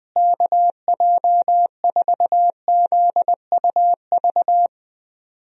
Morse code, heard as a keyed tone, about 20 wpm, 700 Hz. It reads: KJ4ZUV
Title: USA Call Signs - 6 Characters - Repeated 2x - 20wpm